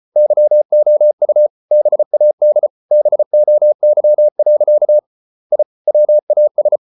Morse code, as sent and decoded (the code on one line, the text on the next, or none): -.-- --- ..- / -... .- -.. / -... --- -.-- .-.-.- / .. / .-- .- ...
YOU BAD BOY. I WAS